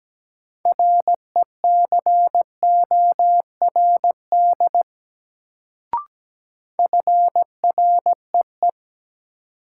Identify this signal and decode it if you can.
Morse code, heard as a keyed tone, about 17 wpm, 700 Hz. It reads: RECORD FREE
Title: Sentences from Top 500 Words - New Word Drill - 17wpm